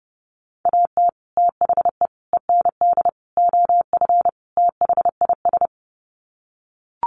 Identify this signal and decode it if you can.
Morse code, heard as a keyed tone, about 30 wpm, 700 Hz. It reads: AT THE END OF THIS
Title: Sets of 5 Words 30wpm